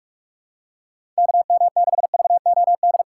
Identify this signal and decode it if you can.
Morse code, heard as a keyed tone, about 43 wpm, 700 Hz. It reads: KM6VOD
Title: USA Callsign DE Callsign 45wpm